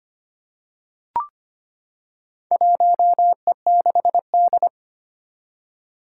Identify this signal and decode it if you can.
Morse code, heard as a keyed tone, about 25 wpm, 700 Hz. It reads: 1E6D
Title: Four Challenging Letter-Number Combinations 25wpm